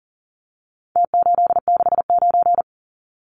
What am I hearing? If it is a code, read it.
Morse code, heard as a keyed tone, about 40 wpm, 700 Hz. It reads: T869
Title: Numbers - Contest Style - 4-Digit 40wpm